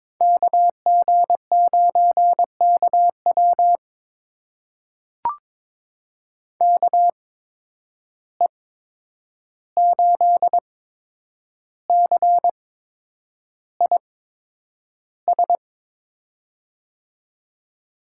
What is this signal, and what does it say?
Morse code, heard as a keyed tone, about 22 wpm, spaced as for 9 wpm, 700 Hz. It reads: KG9KW KE8CIS
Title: USA Call Signs - 8x Character Spacing - 22wpm